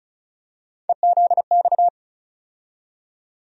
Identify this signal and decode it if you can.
Morse code, heard as a keyed tone, about 36 wpm, 700 Hz. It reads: EZX
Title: USA Call Sign Postfixes - 3 Letters 35wpm